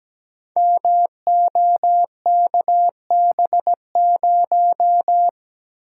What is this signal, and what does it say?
Morse code, heard as a keyed tone, about 17 wpm, 700 Hz. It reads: MOKB0
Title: Code Groups - 5 Letter:Numbers - 17wpm